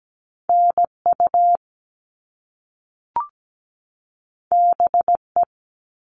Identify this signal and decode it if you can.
Morse code, heard as a keyed tone, about 17 wpm, 700 Hz. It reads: NU BE
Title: All Cut Numbers - 2-Digit 17wpm